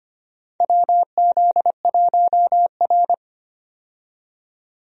Morse code, heard as a keyed tone, about 25 wpm, 700 Hz. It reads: WZ1R
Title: USA Call Signs - 4 Characters - 25wpm